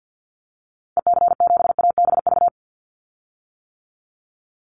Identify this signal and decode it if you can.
Morse code, heard as a keyed tone, about 50 wpm, 700 Hz. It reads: EC7ABV
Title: Call Signs - 50wpm